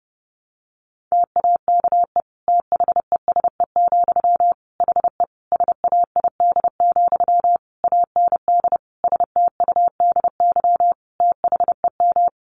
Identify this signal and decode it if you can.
Morse code, heard as a keyed tone, about 30 wpm, 700 Hz. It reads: TAKE THESE, HE SAID, AND STUDY THEM
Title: Sentences from Top 300 Words 30wpm